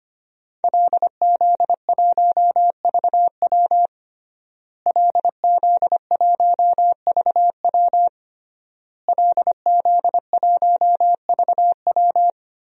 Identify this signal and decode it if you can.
Morse code, heard as a keyed tone, about 25 wpm, 700 Hz. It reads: LZ1VW LZ1VW LZ1VW
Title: DXCC-100 Call Signs - 5 Characters - Repeated 3x - 25wpm